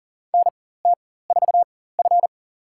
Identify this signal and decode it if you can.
Morse code, heard as a keyed tone, about 40 wpm, spaced as for 19 wpm, 700 Hz. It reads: NT4F
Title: USA Call Signs - 4 Characters - 4x Character Spacing - 40wpm